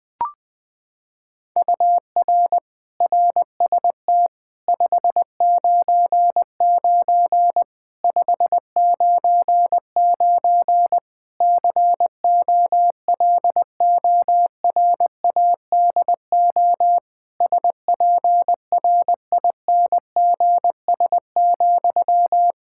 Morse code, heard as a keyed tone, about 21 wpm, 700 Hz. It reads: UR RST 599 599 COLORADO SPRINGS,
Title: K1USN SST QSO Practice 20wpm